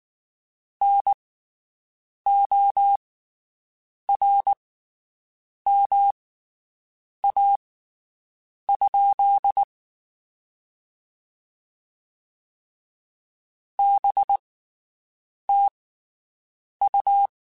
Morse code, heard as a keyed tone, about 19 wpm, spaced as for 7 wpm, 800 Hz. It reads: NORMA? BTU
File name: QSOs Spaced